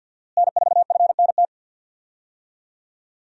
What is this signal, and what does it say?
Morse code, heard as a keyed tone, about 48 wpm, 700 Hz. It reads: N4FNT
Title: USA Call Signs - 5 Characters - Repeated 2x - 50wpm